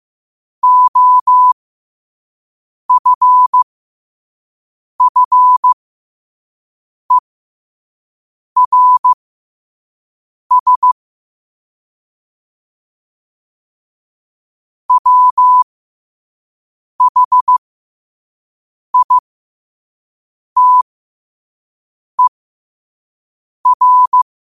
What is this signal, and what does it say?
Morse code, heard as a keyed tone, about 15 wpm, spaced as for 5 wpm, 1000 Hz. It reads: OFFERS WHITER
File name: qotdfiles / 5wpm